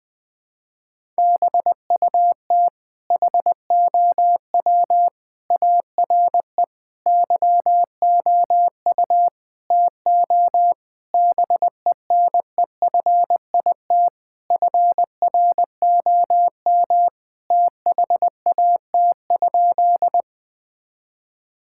Morse code, heard as a keyed tone, about 20 wpm, 700 Hz. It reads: BUT HOW ARE YOU TO BENEFIT FROM THAT?
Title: Sentences from Top 1000 Words - Review Part 6 - 20wpm